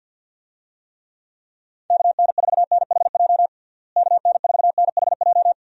Morse code, heard as a keyed tone, about 51 wpm, 700 Hz. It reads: KN4NHJ KN4NHJ
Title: USA Call Signs - 6 Characters - Repeated 2x - 50wpm